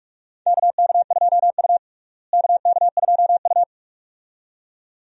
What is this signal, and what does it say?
Morse code, heard as a keyed tone, about 41 wpm, 700 Hz. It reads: KK2U KK2U
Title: USA Call Signs - 4 Characters - Repeated 2x - 45wpm